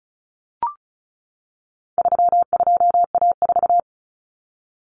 Morse code, heard as a keyed tone, about 34 wpm, 700 Hz. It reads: 32A4
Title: Common Cut Numbers - 4-Digit 35wpm